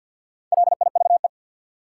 Morse code, heard as a keyed tone, about 53 wpm, 700 Hz. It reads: LIVE
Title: Four Letter Words - 50wpm